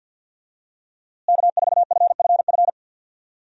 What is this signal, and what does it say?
Morse code, heard as a keyed tone, about 52 wpm, 700 Hz. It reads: K4FFF
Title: USA Call Signs - 5 Characters - Repeated 3x - 50wpm